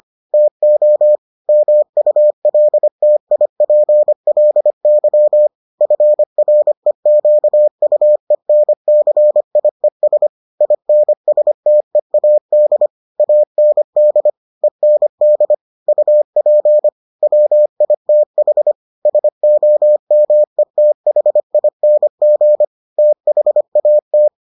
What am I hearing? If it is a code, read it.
Morse code, heard as a keyed tone, about 25 wpm, 600 Hz. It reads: TO MULTIPLY FREQUENCIES INSTEAD AND END UP WITH SOMETHING THAT